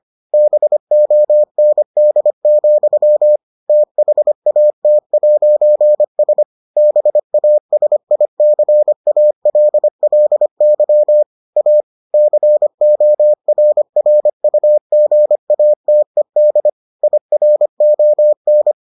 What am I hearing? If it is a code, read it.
Morse code, heard as a keyed tone, about 25 wpm, 600 Hz. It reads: BOND, THAT'S BASICALLY A CORRUGATED IRON